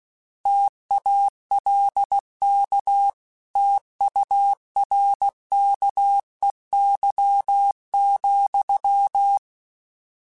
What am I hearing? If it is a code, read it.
Morse code, heard as a keyed tone, about 16 wpm, 800 Hz. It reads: TALK TURKEY,
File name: Misc / Misc-mp3s / Word Pairs @15